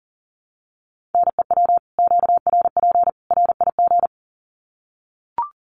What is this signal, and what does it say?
Morse code, heard as a keyed tone, about 40 wpm, 700 Hz. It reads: NEW QRP RIG
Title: Common QSO Phrases 40wpm